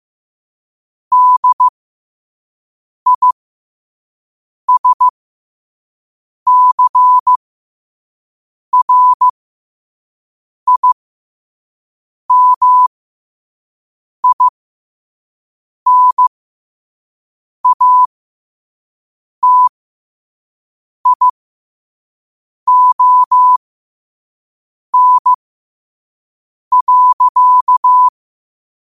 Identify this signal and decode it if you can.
Morse code, heard as a keyed tone, about 15 wpm, spaced as for 5 wpm, 1000 Hz. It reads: DISCRIMINATION.